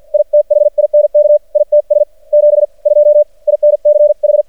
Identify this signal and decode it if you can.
Morse code, heard as a keyed tone, about 50 wpm, spaced as for 41 wpm, 590 Hz. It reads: I THINK ITS 72 INCH
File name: archived / CRSnet